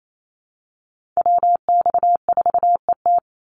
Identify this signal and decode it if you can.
Morse code, heard as a keyed tone, about 28 wpm, 700 Hz. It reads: WX4ET
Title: NA SOTA Chaser Callsigns 28wpm